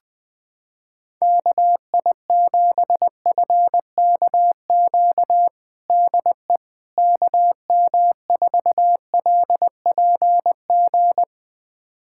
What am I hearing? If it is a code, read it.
Morse code, heard as a keyed tone, about 20 wpm, 700 Hz. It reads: KI7FKQ DE KM4LPG
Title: USA Callsign DE Callsign - 6 Characters - 20wpm